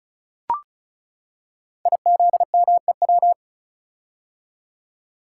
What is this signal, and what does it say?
Morse code, heard as a keyed tone, about 36 wpm, 700 Hz. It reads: IZMEW
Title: Code Groups - 5 Letters - 35wpm